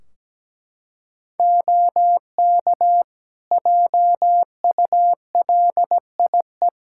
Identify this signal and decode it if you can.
Morse code, heard as a keyed tone, about 17 wpm, 700 Hz. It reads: OK JULIE
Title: Practice QSO - Exchange 3 - More Info - 17wpm